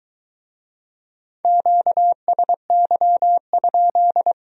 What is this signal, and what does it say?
Morse code, heard as a keyed tone, about 22 wpm, 700 Hz. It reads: QSY?